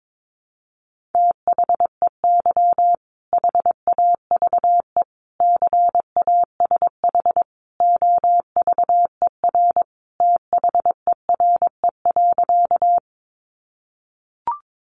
Morse code, heard as a keyed tone, about 22 wpm, 700 Hz. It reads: THEY HAVE CASH OVER THERE.
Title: Sentences from Top 700 Words 22wpm